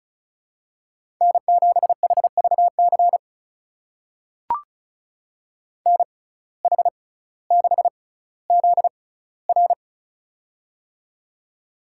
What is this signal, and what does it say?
Morse code, heard as a keyed tone, about 36 wpm, spaced as for 18 wpm, 700 Hz. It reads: N7HVC NH6ZR
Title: USA Call Signs - 5 Characters - 6x Character Spacing - 35wpm